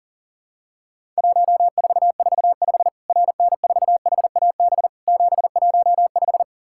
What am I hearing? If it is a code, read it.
Morse code, heard as a keyed tone, about 38 wpm, 700 Hz. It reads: 1445 RN4HAB 715
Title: Worked All Europe DX Contest - QTC Exchange - 40wpm